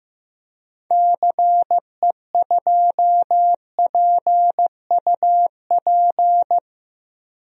Morse code, heard as a keyed tone, about 15 wpm, 700 Hz. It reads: CE2PUP
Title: DXCC-100 Call Signs - 6 Characters - Repeated 3x - 15wpm